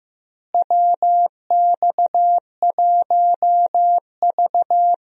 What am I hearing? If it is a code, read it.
Morse code, heard as a keyed tone, about 15 wpm, 700 Hz. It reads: WX1V
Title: USA Call Signs - 4 Characters - Repeated 3x - 15wpm